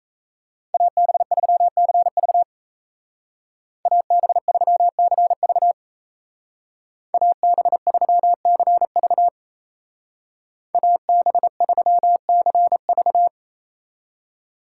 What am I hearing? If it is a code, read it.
Morse code, heard as a keyed tone, about 34 wpm, 700 Hz. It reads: AB3CV AB3CV AB3CV AB3CV
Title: Speed Racing - USA 2x2 Call Signs 28wpm